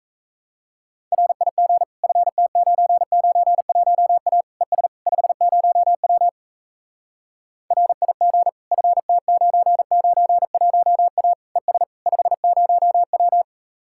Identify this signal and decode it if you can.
Morse code, heard as a keyed tone, about 37 wpm, 700 Hz. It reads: RIG FT991A ES 50W RIG FT991A ES 50W